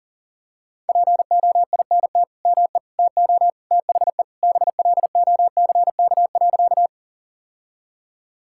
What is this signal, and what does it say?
Morse code, heard as a keyed tone, about 40 wpm, 700 Hz. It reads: POINT ME TO THE BLOCK.
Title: Sentences from Top 1600 Words 40wpm